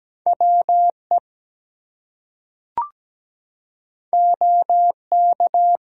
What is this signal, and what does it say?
Morse code, heard as a keyed tone, about 17 wpm, 700 Hz. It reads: WE OK
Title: Two Letter Words - 17wpm